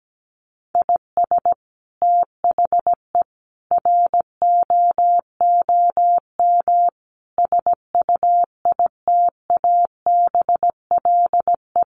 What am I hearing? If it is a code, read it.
Morse code, heard as a keyed tone, about 17 wpm, 700 Hz. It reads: IS THE ROOM SUITABLE
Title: Sentences from Top 1400 Words 17wpm